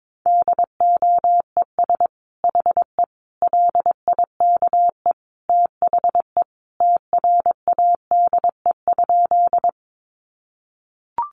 Morse code, heard as a keyed tone, about 22 wpm, 700 Hz. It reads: DOES HE LIKE THE TRADE?
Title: Sentences from Top 600 Words 22wpm